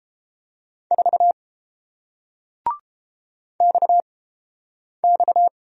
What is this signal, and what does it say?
Morse code, heard as a keyed tone, about 30 wpm, spaced as for 8 wpm, 700 Hz. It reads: 4 XX